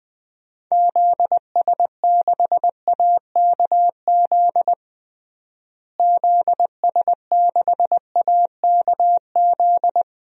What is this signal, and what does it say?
Morse code, heard as a keyed tone, about 20 wpm, 700 Hz. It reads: ZS6AKZ ZS6AKZ
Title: DXCC-100 Call Signs - 6 Characters - Repeated 3x - 20wpm